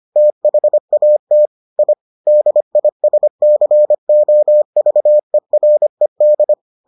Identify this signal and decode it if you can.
Morse code, heard as a keyed tone, about 25 wpm, 600 Hz. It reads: THAT I DISCOVERED